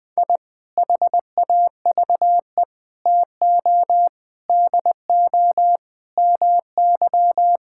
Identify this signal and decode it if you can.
Morse code, heard as a keyed tone, about 21 wpm, 700 Hz. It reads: I HAVE TO DO MY
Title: Sentences with 3-Word Phrases - 20wpm